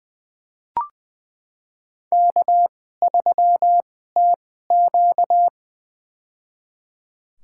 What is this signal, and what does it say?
Morse code, heard as a keyed tone, about 21 wpm, spaced as for 15 wpm, 700 Hz. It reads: K3TQ
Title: USA Call Signs - 4 Characters - 2x Character Spacing - 20wpm